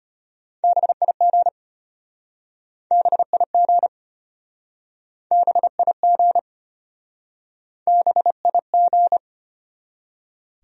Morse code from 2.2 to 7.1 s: BIG BIG

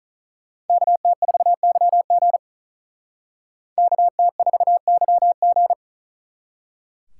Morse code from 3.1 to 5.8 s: KT4YG

